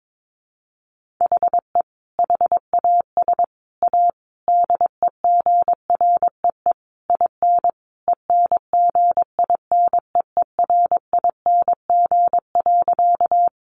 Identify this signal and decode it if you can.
Morse code, heard as a keyed tone, about 22 wpm, 700 Hz. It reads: HE HAS A DEGREE IN ENGINEERING.
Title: Sentences from Top 1500 Words 22wpm